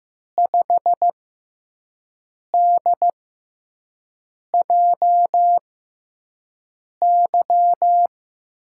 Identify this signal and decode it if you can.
Morse code, heard as a keyed tone, about 15 wpm, spaced as for 5 wpm, 700 Hz. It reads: 5DJY
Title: USA Call Signs - 6 Characters - 6x Character Spacing - 15wpm